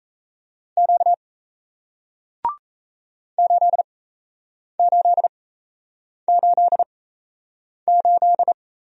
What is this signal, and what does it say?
Morse code, heard as a keyed tone, about 38 wpm, spaced as for 8 wpm, 700 Hz. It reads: Q 8888